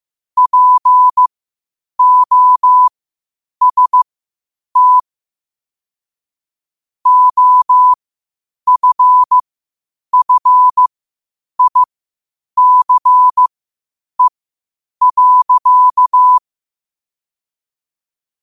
Morse code, heard as a keyed tone, about 15 wpm, spaced as for 8 wpm, 1000 Hz. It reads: POST OFFICE.